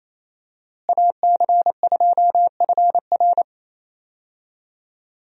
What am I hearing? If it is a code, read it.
Morse code, heard as a keyed tone, about 28 wpm, 700 Hz. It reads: AC2FR
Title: USA Call Signs - 5 Characters - Repeated 3x - 28wpm